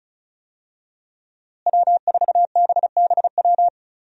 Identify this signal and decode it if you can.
Morse code, heard as a keyed tone, about 36 wpm, 700 Hz. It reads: W4BBW
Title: USA Call Signs - 5 Characters - Repeated 3x - 35wpm